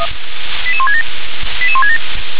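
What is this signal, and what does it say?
Five-tone selective call (ZVEI), heard as quick runs of five tones, tones of about 70 ms. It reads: 88167 88167
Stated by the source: ZVEI five-tone selective call received via SDR.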